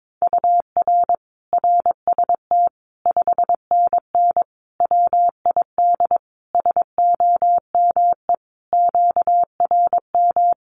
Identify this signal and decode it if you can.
Morse code, heard as a keyed tone, about 22 wpm, 700 Hz. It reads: UR RST 5NN WID SOME QRM
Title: Common QSO Phrases 22wpm